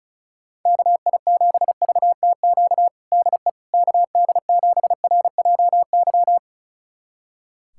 Morse code, heard as a keyed tone, about 34 wpm, 700 Hz. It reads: KI7VTQ DE KD7RJY
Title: USA Callsign DE Callsign - 6 Characters - 35wpm